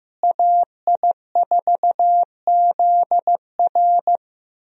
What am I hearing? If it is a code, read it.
Morse code, heard as a keyed tone, about 16 wpm, 700 Hz. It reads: AI4ZR
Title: USA Call Signs - 5 Characters - Repeated 3x - 15wpm